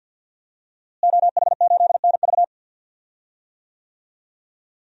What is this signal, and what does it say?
Morse code, heard as a keyed tone, about 51 wpm, 700 Hz. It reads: OH8NV
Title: DXCC-100 Call Signs - 5 Characters - Repeated 3x - 50wpm